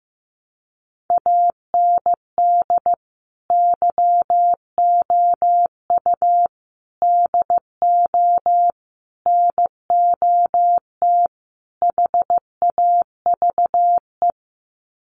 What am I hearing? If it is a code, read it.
Morse code, heard as a keyed tone, about 15 wpm, 700 Hz. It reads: AND YOU DO NOT HAVE
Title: Sets of 5 Words Encore 15wpm